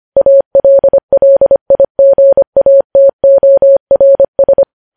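Morse code, heard as a keyed tone, about 25 wpm, 550 Hz. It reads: ALLIGATORS
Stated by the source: Morse code news headlines sent at 25 wpm.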